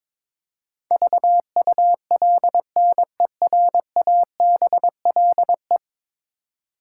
Morse code, heard as a keyed tone, about 22 wpm, 700 Hz. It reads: VULNERABLE
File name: practice / 22wpm / 10 Letter Words - First 3 Letters Given - 22wpm